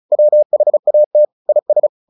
Morse code, heard as a keyed tone, about 34 wpm, 600 Hz. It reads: WHAT IS
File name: the old man and the sea / the old man and the sea 1 35wpm